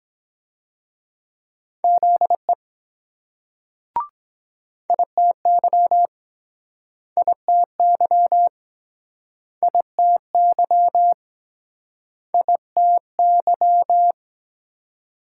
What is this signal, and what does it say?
Morse code, heard as a keyed tone, about 24 wpm, 700 Hz. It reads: ZE ITY ITY ITY ITY